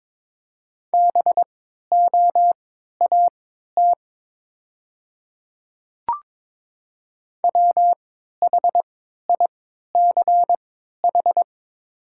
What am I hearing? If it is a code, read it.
Morse code, heard as a keyed tone, about 23 wpm, spaced as for 13 wpm, 700 Hz. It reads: BOAT WHICH
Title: 3x Character Spacing - Top 500 Words - 22wpm